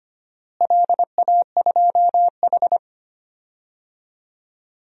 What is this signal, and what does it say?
Morse code, heard as a keyed tone, about 25 wpm, 700 Hz. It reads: LA2H
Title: DXCC-100 Call Signs - 4 Characters - Repeated 2x - 25wpm